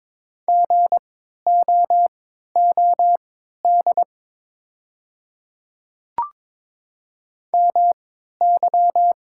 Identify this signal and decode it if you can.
Morse code, heard as a keyed tone, about 22 wpm, spaced as for 13 wpm, 690 Hz. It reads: GOOD MY
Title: 3x Character Spacing - Top 500 Words - 22wpm